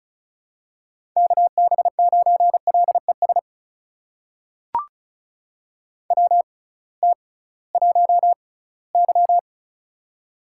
Morse code, heard as a keyed tone, about 36 wpm, spaced as for 20 wpm, 700 Hz. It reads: KB9LES WT1Y
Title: USA Call Signs - 6x Character Spacing - 35wpm